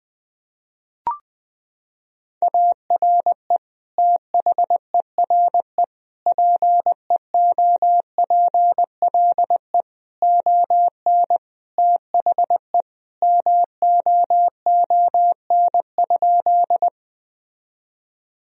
Morse code, heard as a keyed tone, about 20 wpm, 700 Hz. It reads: ARE THERE PEOPLE ON THE MOON?
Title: Sentences from Top 1000 Words - Review Part 3 - 20wpm